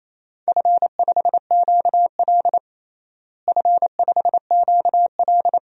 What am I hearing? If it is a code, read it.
Morse code, heard as a keyed tone, about 28 wpm, 700 Hz. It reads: F5QL F5QL
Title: DXCC-100 Call Signs - 4 Characters - Repeated 3x - 28wpm